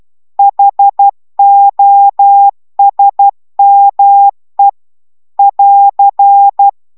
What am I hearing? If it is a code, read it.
Morse code, heard as a keyed tone, about 12 wpm, 800 Hz. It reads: HOSME <AR>